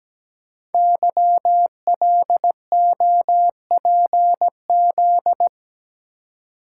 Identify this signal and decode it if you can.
Morse code, heard as a keyed tone, about 17 wpm, 700 Hz. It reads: YLOPZ
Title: Code Groups - 5 Letters - 17wpm